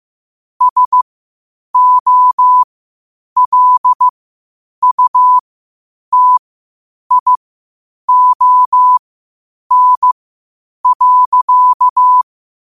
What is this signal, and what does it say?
Morse code, heard as a keyed tone, about 15 wpm, spaced as for 8 wpm, 1000 Hz. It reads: SOLUTION.